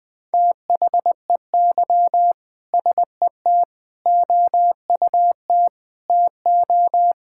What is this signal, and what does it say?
Morse code, heard as a keyed tone, about 21 wpm, 700 Hz. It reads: THEY SET OUT TO
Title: Sentences with 2-Word Phrases - 20wpm